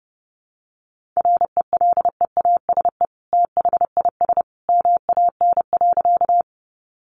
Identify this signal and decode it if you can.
Morse code, heard as a keyed tone, about 30 wpm, 700 Hz. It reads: RELEASE THIS MAN.
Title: Sentences from Top 1100 Words 30wpm